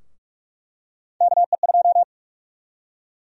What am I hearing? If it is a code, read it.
Morse code, heard as a keyed tone, about 47 wpm, 700 Hz. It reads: KE2
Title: USA Call Sign Prefixes - 45wpm